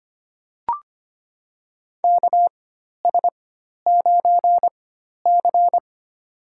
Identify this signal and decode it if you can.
Morse code, heard as a keyed tone, about 25 wpm, spaced as for 12 wpm, 700 Hz. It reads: KS9C
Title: USA Call Signs - 4 Characters - 4x Character Spacing - 25wpm